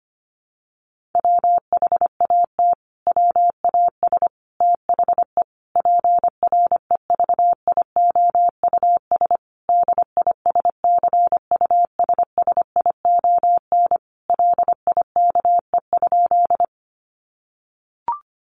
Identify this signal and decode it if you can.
Morse code, heard as a keyed tone, about 25 wpm, 700 Hz. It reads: WHAT WAS THE PREVIOUS DISCUSSION LIKE?
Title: Sentences from Top 1500 Words 25wpm